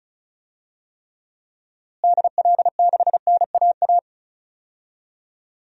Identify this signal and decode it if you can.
Morse code, heard as a keyed tone, about 34 wpm, 700 Hz. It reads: DL6NAA
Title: DXCC-100 Call Signs - 6 Characters - Repeated 3x - 35wpm